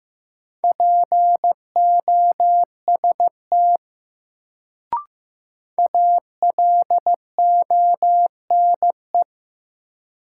Morse code, heard as a keyed tone, about 15 wpm, 700 Hz. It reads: POST ALONE
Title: Sentences from Top 1000 Words - New Word Drill - 15wpm